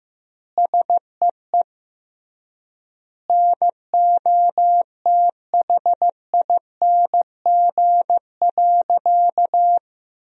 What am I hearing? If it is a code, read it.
Morse code, heard as a keyed tone, about 15 wpm, 700 Hz. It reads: SEE NOTHING.
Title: 3x Word-Spacing - Sentences from Top 500 Words - 15wpm